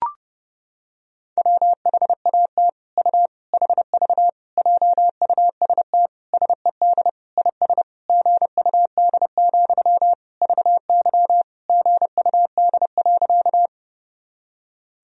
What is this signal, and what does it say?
Morse code, heard as a keyed tone, about 30 wpm, 700 Hz. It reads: WHAT U HV JUST SED IS GUD, VY GUD.
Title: Sentences from Top 200 Words and Abbreviations 30wpm